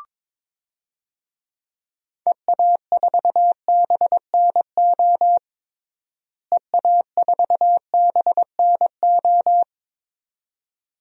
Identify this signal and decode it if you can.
Morse code, heard as a keyed tone, about 22 wpm, 700 Hz. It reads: EA4BNO EA4BNO
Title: DXCC-100 Call Signs - 6 Characters - Repeated 2x - 22wpm